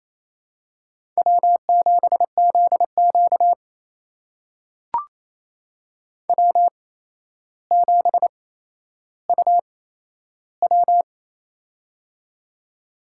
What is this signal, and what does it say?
Morse code, heard as a keyed tone, about 28 wpm, spaced as for 12 wpm, 700 Hz. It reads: W7ZQ W7UW
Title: USA Call Signs - 4 Characters - 8x Character Spacing - 28wpm